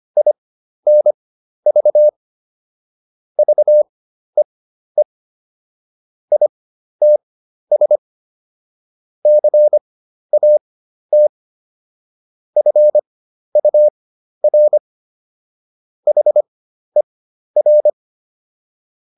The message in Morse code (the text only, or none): INV VEE ITS CAT FUR HER